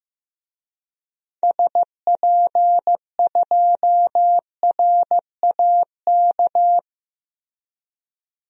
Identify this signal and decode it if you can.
Morse code, heard as a keyed tone, about 15 wpm, 700 Hz. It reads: SP2RAK
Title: DXCC-100 Call Signs - 6 Characters - 15wpm